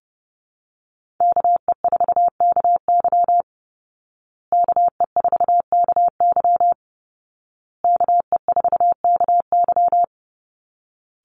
Morse code, heard as a keyed tone, about 30 wpm, 700 Hz. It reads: KE4KY KE4KY KE4KY
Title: Call Signs Repeated 2x - 30wpm